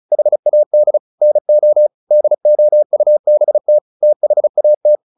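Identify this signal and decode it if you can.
Morse code, heard as a keyed tone, about 34 wpm, 600 Hz. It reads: HAD NO DOUBT THAT
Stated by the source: Morse practice at 35 wpm from CW med Gnister, sending Ken Follett's novel Never in English.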